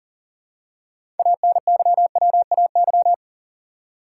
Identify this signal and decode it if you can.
Morse code, heard as a keyed tone, about 40 wpm, 700 Hz. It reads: ANYWAY
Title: Six Letter Words - 40wpm